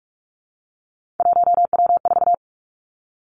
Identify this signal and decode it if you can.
Morse code, heard as a keyed tone, about 46 wpm, 700 Hz. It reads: 1W4
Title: Three Challenging Letter-Number Combinations 45wpm